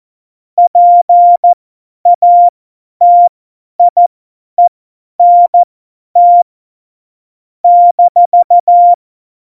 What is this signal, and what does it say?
Morse code, heard as a keyed tone, about 14 wpm, spaced as for 10 wpm, 700 Hz. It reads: PATIENT -
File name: Morsekurs / Witze / Witz 09. Januar 2018, 50 F70